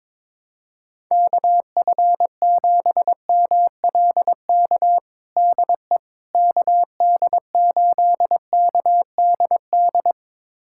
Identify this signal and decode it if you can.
Morse code, heard as a keyed tone, about 22 wpm, 700 Hz. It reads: KF7MLK DE KD8KDD
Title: USA Callsign DE Callsign 22wpm